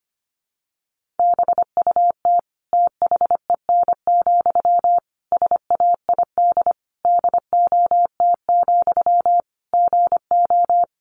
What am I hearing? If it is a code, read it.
Morse code, heard as a keyed tone, about 25 wpm, 700 Hz. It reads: BUT THEN, SAID DOT, GO
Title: Sentences from Top 1500 Words 25wpm